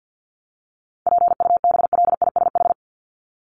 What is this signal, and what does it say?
Morse code, heard as a keyed tone, about 49 wpm, 700 Hz. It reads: PUBLISH
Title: ED Suffix - Top 100 Words with Base Words 50wpm